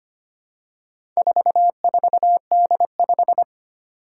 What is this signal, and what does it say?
Morse code, heard as a keyed tone, about 25 wpm, 700 Hz. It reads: 44D5
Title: Four Challenging Letter-Number Combinations 25wpm